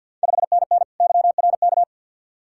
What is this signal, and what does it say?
Morse code, heard as a keyed tone, about 53 wpm, 690 Hz. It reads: 5NN YRK
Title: Ontario QSO Party Exchange 50wpm